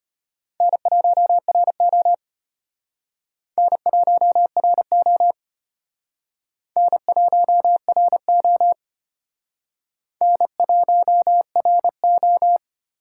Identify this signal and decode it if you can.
Morse code, heard as a keyed tone, about 31 wpm, 700 Hz. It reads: N1RO N1RO N1RO N1RO
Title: Speed Racing - USA 1x2 Call Signs 25wpm